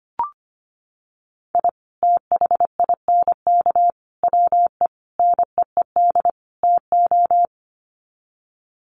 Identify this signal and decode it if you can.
Morse code, heard as a keyed tone, about 25 wpm, 700 Hz. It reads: I THINK WE NEED TO
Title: Sets of 5 Words 25wpm